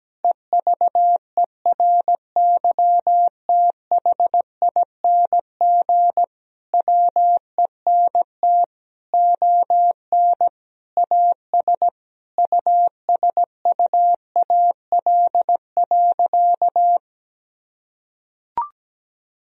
Morse code, heard as a keyed tone, about 17 wpm, 700 Hz. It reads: EVERYTHING WENT ON AS USUAL.
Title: Sentences from Top 1000 Words - Review Part 2 - 17wpm